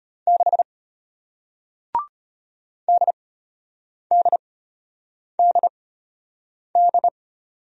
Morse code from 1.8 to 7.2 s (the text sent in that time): DDDD